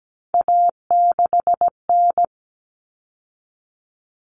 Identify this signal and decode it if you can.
Morse code, heard as a keyed tone, about 17 wpm, 700 Hz. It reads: A6N
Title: All Cut Numbers - 3-Digit 17wpm